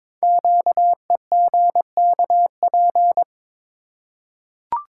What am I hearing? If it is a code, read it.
Morse code, heard as a keyed tone, about 22 wpm, 700 Hz. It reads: QEGKP
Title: Code Groups - 5 Letters - 22wpm